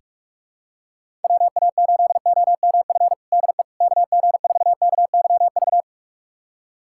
Morse code, heard as a keyed tone, about 46 wpm, 700 Hz. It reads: WA8OMF DE KG4KYV
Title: USA Callsign DE Callsign - 6 Characters - 45wpm